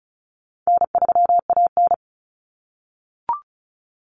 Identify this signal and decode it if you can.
Morse code, heard as a keyed tone, about 34 wpm, 700 Hz. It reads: N3AN
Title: Common Cut Numbers - 4-Digit 35wpm